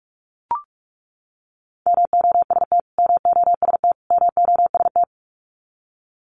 Morse code, heard as a keyed tone, about 44 wpm, 700 Hz. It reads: MOST MOST MOST
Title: Top 200 Words Repeated 45wpm